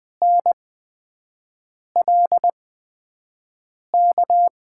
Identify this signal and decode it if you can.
Morse code, heard as a keyed tone, about 20 wpm, spaced as for 5 wpm, 700 Hz. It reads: NLK
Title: USA Call Signs - 5 Characters - 8x Character Spacing - 20wpm